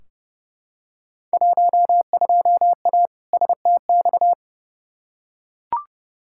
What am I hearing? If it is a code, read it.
Morse code, heard as a keyed tone, about 31 wpm, 700 Hz. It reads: 12A STX
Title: Field Day Exchanges Encore - 30wpm